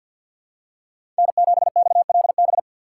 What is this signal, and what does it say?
Morse code, heard as a keyed tone, about 53 wpm, 700 Hz. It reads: N7XLB
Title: USA Call Signs - 5 Characters - 8x Character Spacing - 50wpm